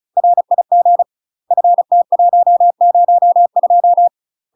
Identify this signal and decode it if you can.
Morse code, heard as a keyed tone, about 36 wpm, 700 Hz. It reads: RIG FT102
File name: UK MASTER QSOs TRAINING/35WPM